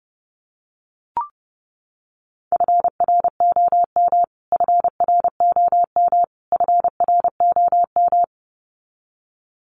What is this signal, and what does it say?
Morse code, heard as a keyed tone, about 30 wpm, 700 Hz. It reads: FROM FROM FROM
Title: Top 500 Words Repeated 30wpm